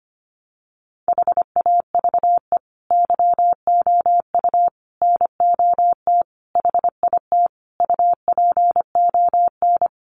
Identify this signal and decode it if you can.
Morse code, heard as a keyed tone, about 25 wpm, 700 Hz. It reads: HAVE YOU NOT HIT UPON
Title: Sentences from Top 1200 Words 25wpm